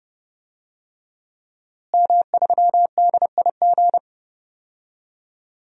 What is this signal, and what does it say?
Morse code, heard as a keyed tone, about 30 wpm, 700 Hz. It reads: M3DIG
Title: DXCC-100 Call Signs - 5 Characters - Repeated 3x - 30wpm